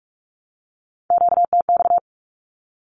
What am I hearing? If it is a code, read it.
Morse code, heard as a keyed tone, about 44 wpm, 700 Hz. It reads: QTX
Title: Q-Codes 45wpm